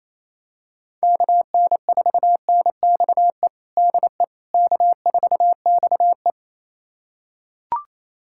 Morse code, heard as a keyed tone, about 28 wpm, 700 Hz. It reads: KN4NXE DE K4XE
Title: USA Callsign DE Callsign 28wpm